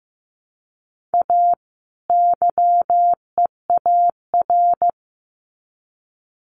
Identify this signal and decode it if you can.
Morse code, heard as a keyed tone, about 15 wpm, 700 Hz. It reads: A YEAR
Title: Sets of 2 Words 15wpm